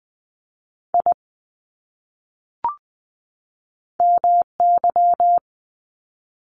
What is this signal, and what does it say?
Morse code, heard as a keyed tone, about 20 wpm, 700 Hz. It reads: I MY
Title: Top 500 Words 20wpm